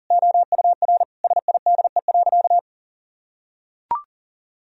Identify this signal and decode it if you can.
Morse code, heard as a keyed tone, about 40 wpm, 700 Hz. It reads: OUR SIDE.
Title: Sentences from Top 1000 Words - Review Part 3 - 40wpm